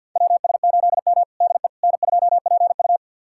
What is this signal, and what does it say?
Morse code, heard as a keyed tone, about 52 wpm, 700 Hz. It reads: WS8M DE N2PU
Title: USA Callsign DE Callsign - 4 Characters - 50wpm